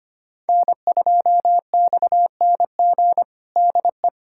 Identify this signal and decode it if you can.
Morse code, heard as a keyed tone, about 25 wpm, 700 Hz. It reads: N2XNG DE
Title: USA Callsign DE Callsign - 5 Characters - 25wpm